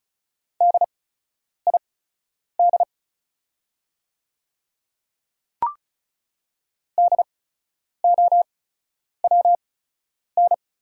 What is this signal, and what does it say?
Morse code, heard as a keyed tone, about 36 wpm, spaced as for 10 wpm, 690 Hz. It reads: DID DOWN
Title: 8x Character Spacing - Top 200 Words - 35wpm